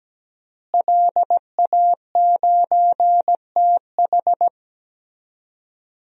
Morse code, heard as a keyed tone, about 17 wpm, 700 Hz. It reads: LA9TH
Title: DXCC-100 Call Signs - 5 Characters - Repeated 3x - 17wpm